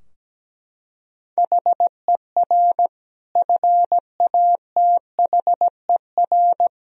Morse code, heard as a keyed tone, about 17 wpm, 700 Hz. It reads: HER FATHER
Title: Sentences from Top 1000 Words - Review Part 3 - 17wpm